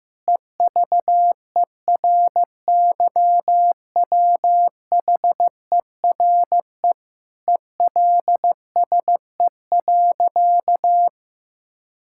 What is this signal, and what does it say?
Morse code, heard as a keyed tone, about 15 wpm, 700 Hz. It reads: EVERYWHERE ELSE.
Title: Sentences from Top 1600 Words 15wpm